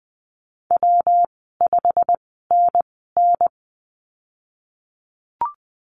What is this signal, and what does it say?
Morse code, heard as a keyed tone, about 20 wpm, 700 Hz. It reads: W 5 N N
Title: 154 Lesson 25g - Letter p - Call Signs 20wpm